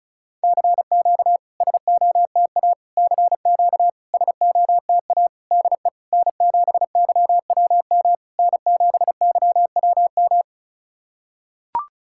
Morse code, heard as a keyed tone, about 36 wpm, 700 Hz. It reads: CQ SOTA CQ SOTA DE N7YWM N7YWM
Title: SOTA QSO Practice 35wpm